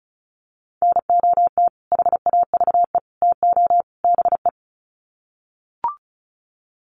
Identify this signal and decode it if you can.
Morse code, heard as a keyed tone, about 34 wpm, 700 Hz. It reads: NOT HAVE TO BE
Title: Sets of 4 Words 35wpm